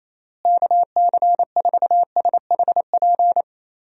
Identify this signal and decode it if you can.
Morse code, heard as a keyed tone, about 28 wpm, 700 Hz. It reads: KC4SHP
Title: USA Call Signs - 6 Characters - Repeated 3x - 28wpm